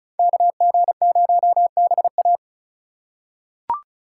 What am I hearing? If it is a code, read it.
Morse code, heard as a keyed tone, about 34 wpm, 700 Hz. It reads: KG0BA
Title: USA Call Signs - 5 Characters - 2x Character Spacing - 35wpm